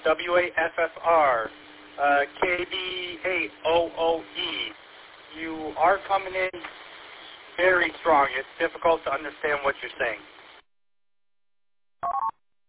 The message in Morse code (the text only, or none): none